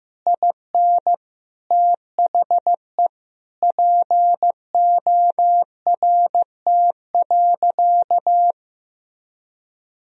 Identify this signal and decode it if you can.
Morse code, heard as a keyed tone, about 15 wpm, 700 Hz. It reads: IN THE PORT.
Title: Sentences from Top 1000 Words - Review Part 1 - 15wpm